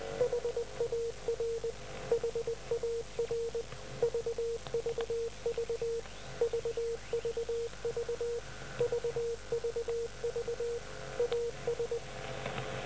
Morse code, heard as a keyed tone, about 20 wpm, 470 Hz. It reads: HAR HAR VVV VVV VVV AS